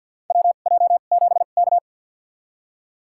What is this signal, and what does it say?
Morse code, heard as a keyed tone, about 46 wpm, spaced as for 35 wpm, 700 Hz. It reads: WJ7K
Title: USA Call Signs - 2x Character Spacing - 50wpm